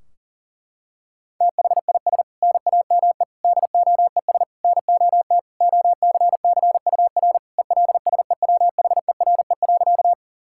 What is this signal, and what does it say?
Morse code, heard as a keyed tone, about 40 wpm, 700 Hz. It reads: THIS NAME DOES NOT OCCUR ELSEWHERE.
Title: Sentences from Top 1600 Words 40wpm